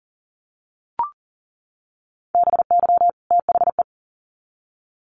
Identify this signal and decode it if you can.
Morse code, heard as a keyed tone, about 40 wpm, 700 Hz. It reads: BY THE
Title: Sets of 2 Words 40wpm